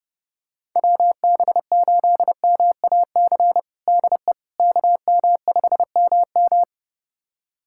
Morse code, heard as a keyed tone, about 32 wpm, 700 Hz. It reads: WB8MAC DE KM5MM
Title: USA Callsign DE Callsign 30wpm